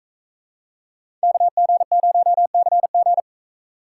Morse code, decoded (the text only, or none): KG0CG